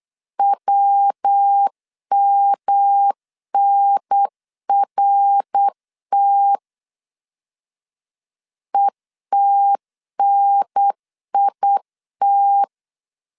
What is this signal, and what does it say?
Morse code, heard as a keyed tone, about 8 wpm, 800 Hz. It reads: WMNRT ETNIT